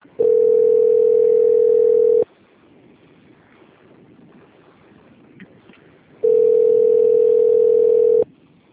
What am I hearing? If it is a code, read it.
Ringback tone (North America), 440 + 480 Hz.